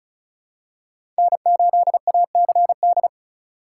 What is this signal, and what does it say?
Morse code, heard as a keyed tone, about 36 wpm, 700 Hz. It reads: N8ACD
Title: USA Call Signs - 3x Character Spacing - 35wpm